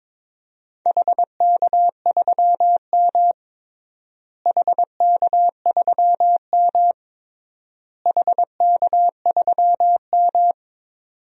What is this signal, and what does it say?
Morse code, heard as a keyed tone, about 22 wpm, 700 Hz. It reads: HK3M HK3M HK3M
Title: DXCC-100 Call Signs - 4 Characters - Repeated 3x - 22wpm